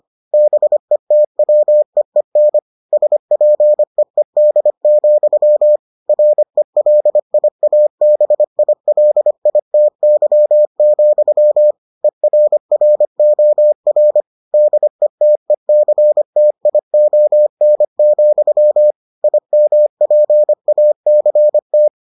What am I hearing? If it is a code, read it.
Morse code, heard as a keyed tone, about 25 wpm, 600 Hz. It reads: BETWEEN SPEED, RELIABILITY, ERROR DETECTION, IMPACT